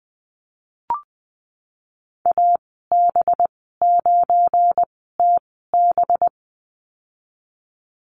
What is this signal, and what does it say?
Morse code, heard as a keyed tone, about 20 wpm, spaced as for 14 wpm, 700 Hz. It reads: AB9TB
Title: 238 Lesson 38g - Letter q - Call Signs 20wpm